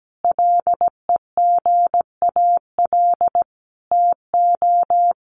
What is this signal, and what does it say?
Morse code, heard as a keyed tone, about 17 wpm, 700 Hz. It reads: LEGAL TO
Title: Sentences from Top 1200 Words 17wpm